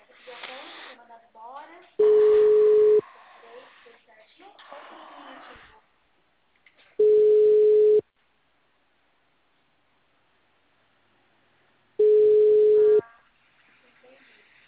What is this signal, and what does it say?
Ringback tone (Europe, 425 Hz), with speech.